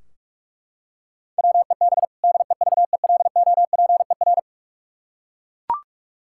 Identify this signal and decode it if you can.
Morse code, heard as a keyed tone, about 42 wpm, 700 Hz. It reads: WEB DEVELOPER
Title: Practice QSO - Exchange 3 - More Info - 45wpm